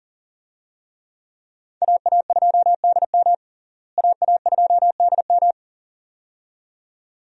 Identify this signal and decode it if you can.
Morse code, heard as a keyed tone, about 40 wpm, 700 Hz. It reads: AA2DM AA2DM
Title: USA Call Signs - 5 Characters - Repeated 2x - 40wpm